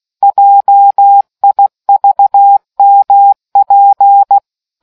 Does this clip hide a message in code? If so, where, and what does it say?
Yes: Morse code from 0.2 to 4.4 s, heard as a keyed tone, about 16 wpm, 800 Hz. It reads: JIVMP